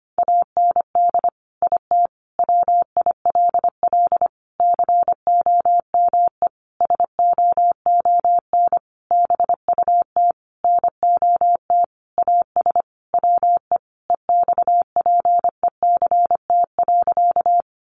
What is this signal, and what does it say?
Morse code, heard as a keyed tone, about 25 wpm, 700 Hz. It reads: AND IT WILL COME SOON BUT NOT AS WE EXPECT.
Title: Sentences from Top 1100 Words 25wpm